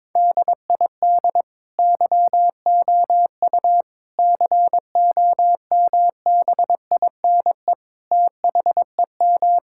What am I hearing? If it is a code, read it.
Morse code, heard as a keyed tone, about 22 wpm, 700 Hz. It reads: DID YOU COMBINE THEM
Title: Sentences from Top 1600 Words 22wpm